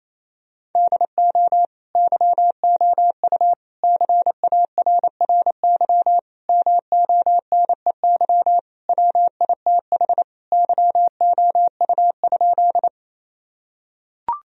Morse code, heard as a keyed tone, about 28 wpm, 700 Hz. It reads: DO YOU CARRY MONEY WITH YOU?
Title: Sentences from Top 1000 Words - Review Part 1 - 28wpm